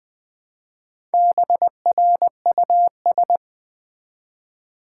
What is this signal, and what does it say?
Morse code, heard as a keyed tone, about 20 wpm, 700 Hz. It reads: BRUS
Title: Build A Word - 1 Letter At A Time - 5-Letter Words - 20wpm